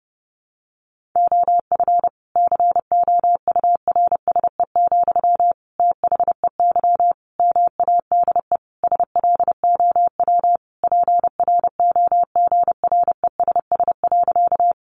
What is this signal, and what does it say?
Morse code, heard as a keyed tone, about 30 wpm, 700 Hz. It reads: OF COURSE, THEY MADE SLOW PROGRESS.
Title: Sentences from Top 1400 Words 30wpm